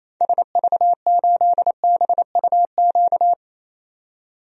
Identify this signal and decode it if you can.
Morse code, heard as a keyed tone, about 28 wpm, 700 Hz. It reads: SV8BUQ
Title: DXCC-100 Call Signs - 6 Characters - 28wpm